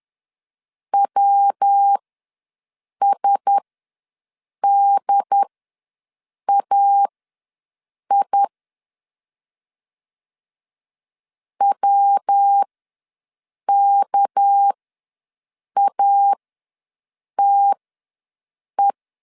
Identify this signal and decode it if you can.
Morse code, heard as a keyed tone, about 10 wpm, spaced as for 6 wpm, 800 Hz. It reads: WSDAI WKATE
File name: LEZIONI / 6 PKX0 / 6 PKX ws 10wpm 3706